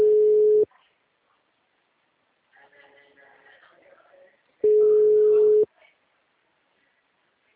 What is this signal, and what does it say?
Ringback tone (Europe, 425 Hz), with speech.
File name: out- -2003- - - .6